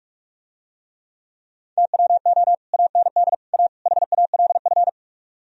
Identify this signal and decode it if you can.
Morse code, heard as a keyed tone, about 46 wpm, 700 Hz. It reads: TWO AND A HALF